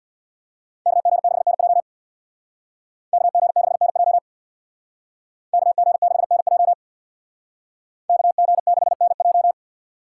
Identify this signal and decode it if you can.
Morse code, heard as a keyed tone, about 63 wpm, 700 Hz. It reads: KG6NJ KG6NJ KG6NJ KG6NJ